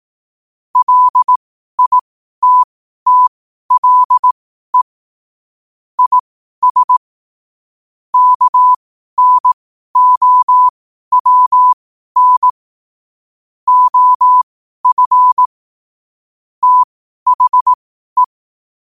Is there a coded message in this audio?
Yes: Morse code from 0.8 to 18.2 s, heard as a keyed tone, about 18 wpm, spaced as for 12 wpm, 1000 Hz. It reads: LITTLE IS KNOWN OF THE